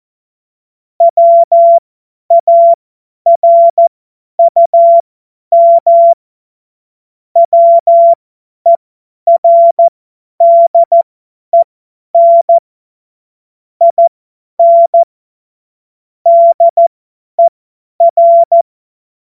Morse code, heard as a keyed tone, about 14 wpm, spaced as for 10 wpm, 680 Hz. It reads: WARUM WERDEN IN DER